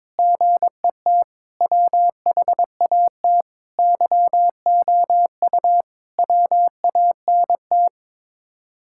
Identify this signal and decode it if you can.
Morse code, heard as a keyed tone, about 22 wpm, 700 Hz. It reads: GET WHAT YOU WANT
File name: practice / 22wpm / Sets of 7 Words - 22wpm